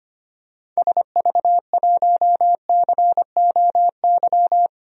Morse code, heard as a keyed tone, about 25 wpm, 700 Hz. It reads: SV1COY
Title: DXCC-100 Call Signs - 6 Characters - 25wpm